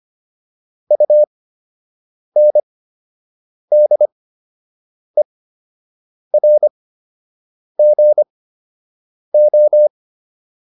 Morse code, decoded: UNDERGO

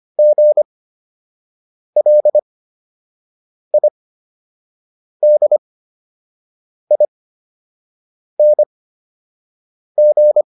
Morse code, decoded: GLIDING

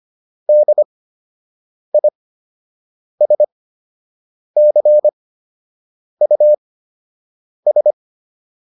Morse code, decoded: DISCUS

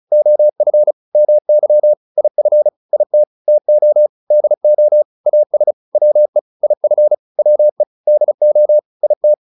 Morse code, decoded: OF MY IF IT TO DO AS WE IF WE DO IT

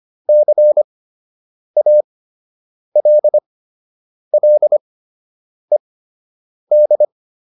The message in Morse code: CALLED

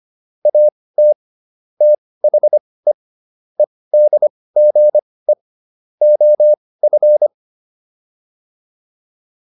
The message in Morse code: AT THE EDGE OF

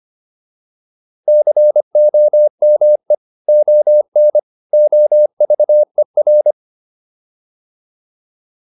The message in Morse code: COME ON OVER

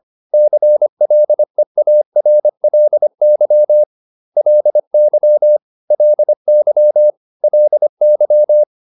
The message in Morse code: CLEARLY LY LY LY

